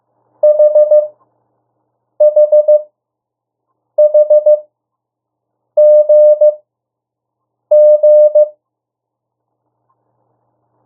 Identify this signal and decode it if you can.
Morse code, heard as a keyed tone, about 15 wpm, spaced as for 6 wpm, 600 Hz. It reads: HHHGG
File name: urok 1